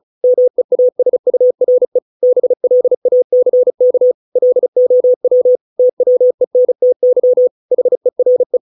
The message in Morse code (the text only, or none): MEASURE BLACK LOW TWENTY HERE